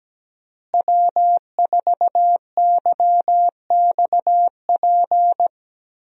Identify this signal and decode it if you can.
Morse code, heard as a keyed tone, about 17 wpm, 700 Hz. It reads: W4YXP
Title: USA Call Signs - 5 Characters - Repeated 3x - 17wpm